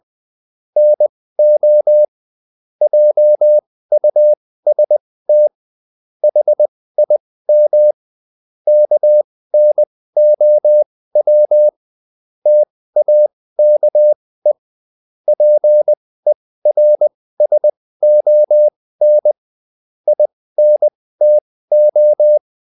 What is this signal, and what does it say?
Morse code, heard as a keyed tone, about 20 wpm, spaced as for 15 wpm, 600 Hz. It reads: NO JUST HIM KNOW TAKE PERSON INTO
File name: Top-100-Words-15